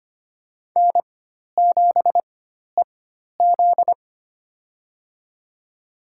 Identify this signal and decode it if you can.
Morse code, heard as a keyed tone, about 25 wpm, spaced as for 12 wpm, 700 Hz. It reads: N7EZ
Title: USA Call Signs - 4 Characters - 4x Character Spacing - 25wpm